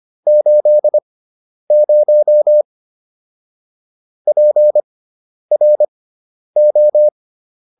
Morse code, heard as a keyed tone, about 25 wpm, spaced as for 10 wpm, 600 Hz. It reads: 80 PRO